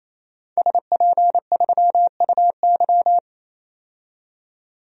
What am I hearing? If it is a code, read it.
Morse code, heard as a keyed tone, about 28 wpm, 700 Hz. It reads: SP3UY
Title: DXCC-100 Call Signs - 5 Characters - 28wpm